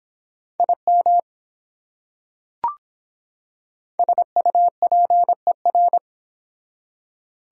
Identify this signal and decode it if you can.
Morse code, heard as a keyed tone, about 25 wpm, 700 Hz. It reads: IM SUPER